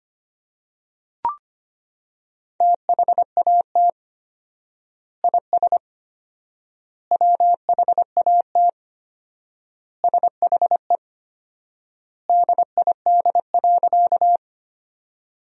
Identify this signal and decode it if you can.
Morse code, heard as a keyed tone, about 25 wpm, 700 Hz. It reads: THAT IS WHAT SHE DID.
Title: 4x Word-Spacing - Sentences from Top 100 Words - 25wpm